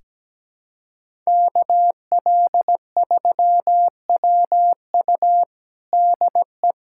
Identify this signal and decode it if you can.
Morse code, heard as a keyed tone, about 17 wpm, 700 Hz. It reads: KL3WU DE
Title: USA Callsign DE Callsign 17wpm